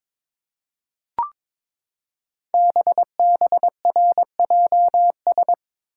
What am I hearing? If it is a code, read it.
Morse code, heard as a keyed tone, about 22 wpm, 700 Hz. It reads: BBRJS
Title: Code Groups - 5 Letters - 22wpm